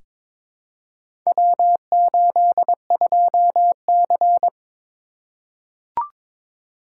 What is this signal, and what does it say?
Morse code, heard as a keyed tone, about 22 wpm, 700 Hz. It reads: W82C
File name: practice / 22wpm / Four Letter-Number Combinations - 22wpm